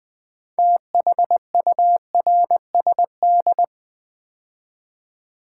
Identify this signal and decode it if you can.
Morse code, heard as a keyed tone, about 20 wpm, 700 Hz. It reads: THURSD
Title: Build A Word - 3 Letters At A Time - 8-Letter Words - 20wpm